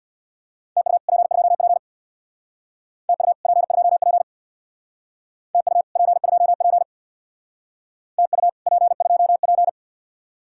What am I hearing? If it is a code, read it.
Morse code, heard as a keyed tone, about 63 wpm, 700 Hz. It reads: TU P2P TU P2P TU P2P TU P2P